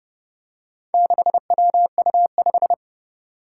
Morse code, heard as a keyed tone, about 29 wpm, 700 Hz. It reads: 6WU5
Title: Four Challenging Letter-Number Combinations 30wpm